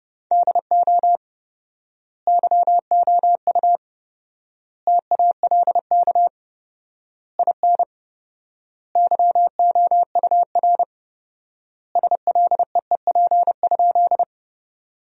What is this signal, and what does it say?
Morse code, heard as a keyed tone, about 31 wpm, 700 Hz. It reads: DO YOU TALK IN YOUR SLEEP?
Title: 4x Word-Spacing - Sentences from Top 500 Words - 30wpm